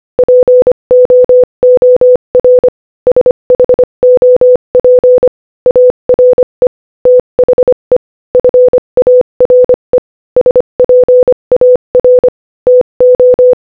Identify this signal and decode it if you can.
Morse code, heard as a keyed tone, about 25 wpm, 500 Hz. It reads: POOR SHOP ARE THE FARE SPAR TO
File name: Morse-Code-Audio-Speed-25-2